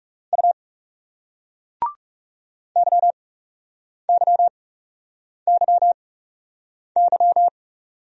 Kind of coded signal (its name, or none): Morse code